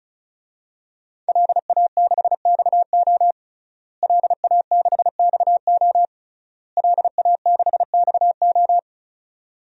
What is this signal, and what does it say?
Morse code, heard as a keyed tone, about 36 wpm, 700 Hz. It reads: LA6XO LA6XO LA6XO
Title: DXCC-100 Call Signs - 5 Characters - Repeated 3x - 35wpm